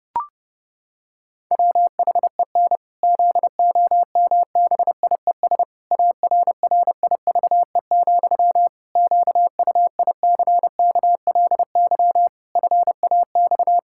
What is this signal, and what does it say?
Morse code, heard as a keyed tone, about 30 wpm, 700 Hz. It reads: WHEN ZOMBIES ARRIVE, QUICKLY FAX
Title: Pangram Challenge v2 - 30wpm